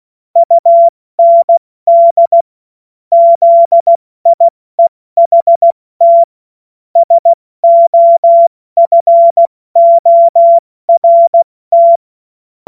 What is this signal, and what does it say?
Morse code, heard as a keyed tone, about 16 wpm, 680 Hz. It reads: UND ZIEHT SOFORT
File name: Morsekurs / Witze / Witz 07. Dezember 2021, 70 F80